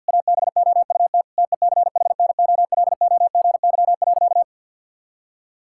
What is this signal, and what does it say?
Morse code, heard as a keyed tone, about 50 wpm, 700 Hz. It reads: ABOUT TECHNOLOGY.
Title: Sentences from Top 1000 Words - Review Part 5 - 50wpm